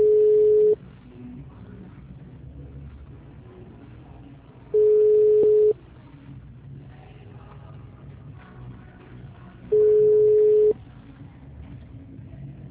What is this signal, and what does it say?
Ringback tone (Europe, 425 Hz).